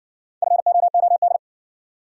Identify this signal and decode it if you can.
Morse code, heard as a keyed tone, about 60 wpm, 700 Hz. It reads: FOOD